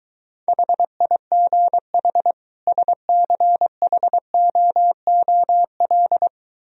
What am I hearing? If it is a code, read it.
Morse code, heard as a keyed tone, about 23 wpm, 700 Hz. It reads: HIGH SCHOOL